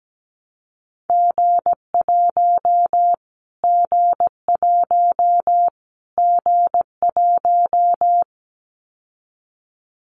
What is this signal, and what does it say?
Morse code, heard as a keyed tone, about 17 wpm, 700 Hz. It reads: G1 G1 G1
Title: Callsign Prefixes for Top 25 DXCC - Repeated 3x - 17wpm